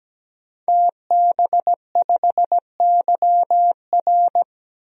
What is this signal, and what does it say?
Morse code, heard as a keyed tone, about 17 wpm, 700 Hz. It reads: TB5YR
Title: DXCC-100 Call Signs - 5 Characters - 17wpm